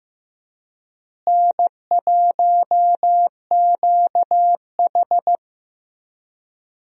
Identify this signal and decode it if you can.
Morse code, heard as a keyed tone, about 15 wpm, 700 Hz. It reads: N1QH
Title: USA Call Signs - 4 Characters - Repeated 3x - 15wpm